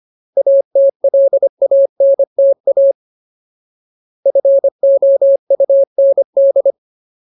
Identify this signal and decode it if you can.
Morse code, heard as a keyed tone, about 25 wpm, 550 Hz. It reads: ATLANTA FOUND
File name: Ultimate Word List / 25WPM Ultimate